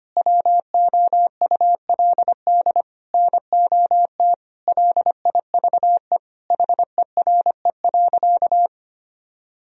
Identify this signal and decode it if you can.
Morse code, heard as a keyed tone, about 25 wpm, 700 Hz. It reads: WOULD NOT LIVE HERE.
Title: Sentences from Top 1000 Words - Review Part 1 - 25wpm